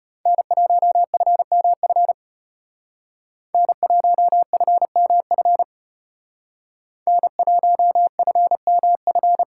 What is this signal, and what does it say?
Morse code, heard as a keyed tone, about 33 wpm, 700 Hz. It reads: N1FMF N1FMF N1FMF